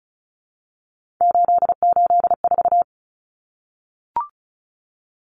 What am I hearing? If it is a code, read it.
Morse code, heard as a keyed tone, about 34 wpm, 700 Hz. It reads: 884
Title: Common Cut Numbers - 3-Digit 35wpm